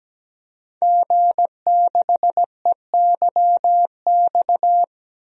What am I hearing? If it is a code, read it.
Morse code, heard as a keyed tone, about 17 wpm, 700 Hz. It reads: G6EYX
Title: DXCC-100 Call Signs - 5 Characters - Repeated 3x - 17wpm